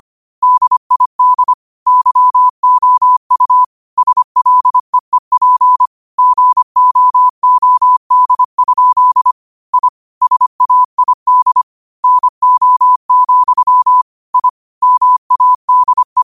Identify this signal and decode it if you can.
Morse code, heard as a keyed tone, about 25 wpm, 1000 Hz. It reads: DID YOU SLEEP GOOD? I SAID NO, I MADE